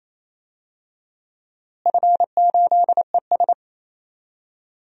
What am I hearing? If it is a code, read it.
Morse code, heard as a keyed tone, about 28 wpm, 700 Hz. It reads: F8ES
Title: DXCC-100 Call Signs - 4 Characters - Repeated 2x - 28wpm